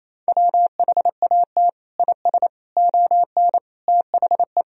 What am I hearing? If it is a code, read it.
Morse code, heard as a keyed tone, about 28 wpm, 700 Hz. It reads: WHAT IS ON THE
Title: Sentences from Top 1000 Words - Review Part 2 - 28wpm